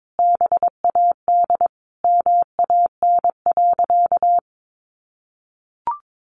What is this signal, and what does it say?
Morse code, heard as a keyed tone, about 22 wpm, 700 Hz. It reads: BAD MAN.
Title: Sentences from Top 1200 Words 22wpm